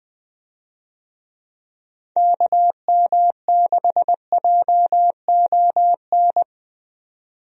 Morse code, heard as a keyed tone, about 20 wpm, 700 Hz. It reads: KM6JON
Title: USA Call Signs - 6 Characters - 20wpm